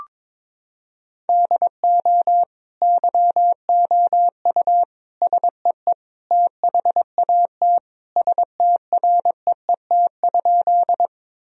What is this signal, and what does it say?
Morse code, heard as a keyed tone, about 22 wpm, 700 Hz. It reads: DO YOU SEE THAT STREET?
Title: Sentences from Top 1000 Words - Review Part 3 - 22wpm